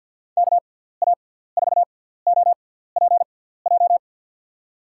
Morse code, heard as a keyed tone, about 52 wpm, spaced as for 17 wpm, 700 Hz. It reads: KA4OPJ
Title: USA Call Signs - 6 Characters - 6x Character Spacing - 50wpm